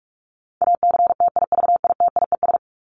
Morse code, heard as a keyed tone, about 45 wpm, 700 Hz. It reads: ACTIVITIES
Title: ES:S Suffix - Top 100 Words with Base Words 45wpm